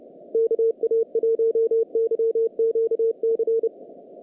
Morse code, heard as a keyed tone, about 30 wpm, 450 Hz, with radio noise. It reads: KA1YQC
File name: contests / naqp-cw-k5tr-op-k5pi- -1800